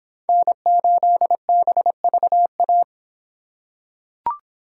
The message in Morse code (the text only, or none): N8BVA